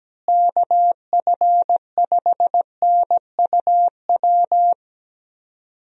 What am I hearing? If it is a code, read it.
Morse code, heard as a keyed tone, about 17 wpm, 700 Hz. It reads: KF5NUW
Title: USA Call Signs - 6 Characters - Repeated 2x - 17wpm